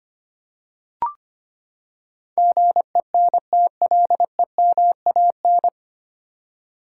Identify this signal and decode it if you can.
Morse code, heard as a keyed tone, about 25 wpm, 700 Hz. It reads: GENTLEMAN
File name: Nine Letter Words - 25wpm